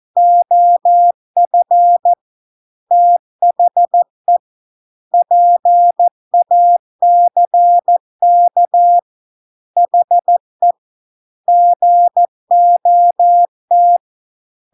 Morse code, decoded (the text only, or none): OF THE PACK HE GOT